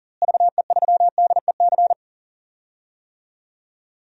Morse code, heard as a keyed tone, about 40 wpm, 700 Hz. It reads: VE3DEC
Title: DXCC-100 Call Signs - 6 Characters - Repeated 3x - 40wpm